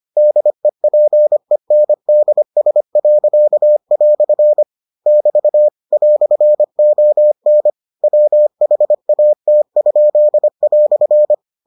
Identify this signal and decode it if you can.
Morse code, heard as a keyed tone, about 25 wpm, 600 Hz. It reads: DEPENDS." <BT> "ON WHAT?"